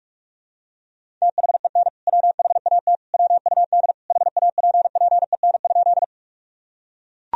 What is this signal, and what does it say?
Morse code, heard as a keyed tone, about 41 wpm, 700 Hz. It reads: THEN WHAT WUD HAPPEN?
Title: Sentences from Top 400 Words and Abbreviations 45wpm